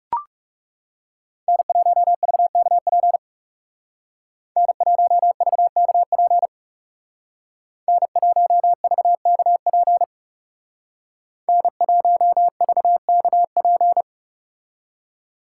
Morse code, heard as a keyed tone, about 39 wpm, 700 Hz. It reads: N1VKP N1VKP N1VKP N1VKP